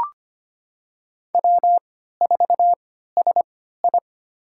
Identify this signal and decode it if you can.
Morse code, heard as a keyed tone, about 25 wpm, spaced as for 14 wpm, 700 Hz. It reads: W4SI
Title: USA Call Signs - 4 Characters - 3x Character Spacing - 25wpm